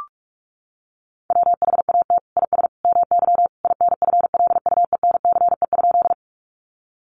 Morse code, heard as a keyed tone, about 44 wpm, 700 Hz. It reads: WHAT IS MY INFLUENCE?
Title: Sentences from Top 1100 Words 45wpm